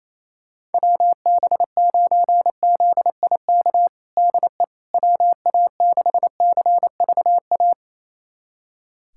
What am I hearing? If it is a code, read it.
Morse code, heard as a keyed tone, about 28 wpm, 700 Hz. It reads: WB9ZIK DE WA6CVA
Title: USA Callsign DE Callsign - 6 Characters - 28wpm